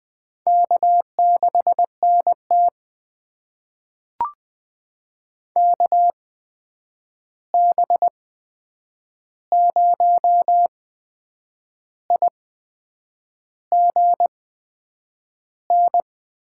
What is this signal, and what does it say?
Morse code, heard as a keyed tone, about 20 wpm, spaced as for 8 wpm, 700 Hz. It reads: K6NT KB0IGN
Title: USA Call Signs - 8x Character Spacing - 20wpm